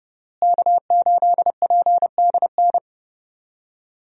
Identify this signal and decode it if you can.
Morse code, heard as a keyed tone, about 30 wpm, 700 Hz. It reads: K8PDN
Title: USA Call Signs - 5 Characters - Repeated 2x - 30wpm